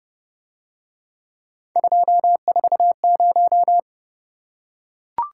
Morse code, heard as a keyed tone, about 29 wpm, 700 Hz. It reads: 240